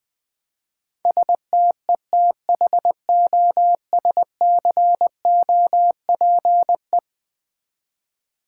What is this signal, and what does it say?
Morse code, heard as a keyed tone, about 20 wpm, 700 Hz. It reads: STETHOSCOPE
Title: Easy Words with Hints - 20wpm